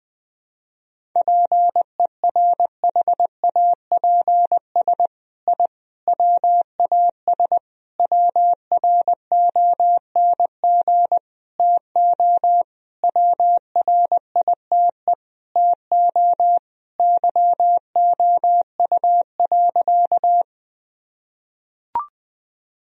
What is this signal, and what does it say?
Morse code, heard as a keyed tone, about 20 wpm, 700 Hz. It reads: PERHAPS I WAS WRONG TO WRITE TO YOU.
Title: Sentences from Top 1000 Words - Review Part 1 - 20wpm